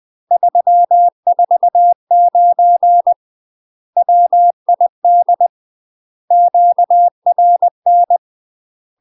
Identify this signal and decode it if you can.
Morse code, heard as a keyed tone, about 20 wpm, 700 Hz. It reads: 349 WID QRN
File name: UK MASTER QSOs TRAINING / 20WPM / UK MASTER QSOs 0063